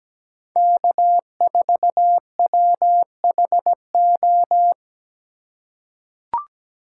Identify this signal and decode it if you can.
Morse code, heard as a keyed tone, about 17 wpm, 700 Hz. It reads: K4WHO
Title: USA Call Signs - 4x Character Spacing - 17wpm